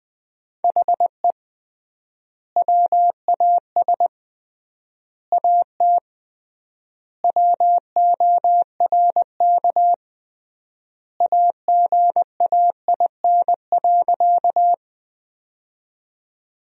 Morse code, heard as a keyed tone, about 21 wpm, 700 Hz. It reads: HE WAS AT WORK AGAIN.
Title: 3x Word-Spacing - Sentences from Top 200 Words - 20wpm